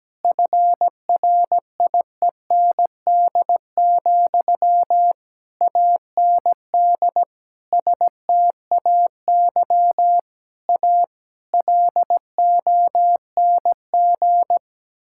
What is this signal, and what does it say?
Morse code, heard as a keyed tone, about 17 wpm, 700 Hz. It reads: FRIEND, AND STAY A LONG
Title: Sentences from Top 1000 Words - Review Part 6 - 17wpm